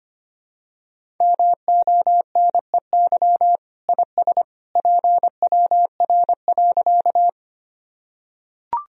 Morse code, heard as a keyed tone, about 25 wpm, 700 Hz. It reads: MONEY IS PWR.
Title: Sentences from Top 500 Words and Abbreviations 25wpm